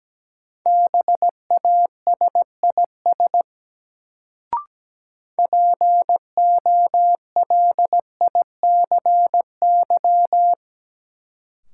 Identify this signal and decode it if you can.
Morse code, heard as a keyed tone, about 17 wpm, 700 Hz. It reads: BASIS POLICY
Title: Sentences from Top 1000 Words - Review Part 5 - New Word Drill - 17wpm